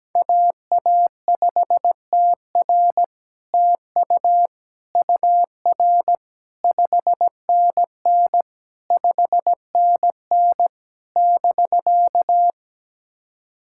Morse code, heard as a keyed tone, about 17 wpm, 700 Hz. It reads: AA5TR TU UR 5NN 5NN <BK>
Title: SOTA QSO Practice 17wpm